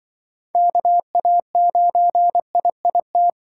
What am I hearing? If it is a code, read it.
Morse code, heard as a keyed tone, about 24 wpm, 700 Hz. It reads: KA9IIT
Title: Speed Racing - USA 2x3 Call Signs 20wpm